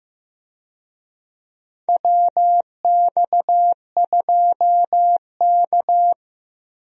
Morse code, heard as a keyed tone, about 15 wpm, 700 Hz. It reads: WX2K
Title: USA Call Signs - 4 Characters - 15wpm